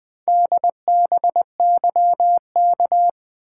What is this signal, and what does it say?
Morse code, heard as a keyed tone, about 24 wpm, 700 Hz. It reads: DBYK
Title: Speed-Racing - Four Letter-Number Combinations - 20wpm